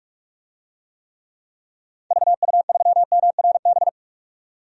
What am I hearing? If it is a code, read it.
Morse code, heard as a keyed tone, about 43 wpm, 700 Hz. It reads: VU3MRB
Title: DXCC-100 Call Signs - 6 Characters - 45wpm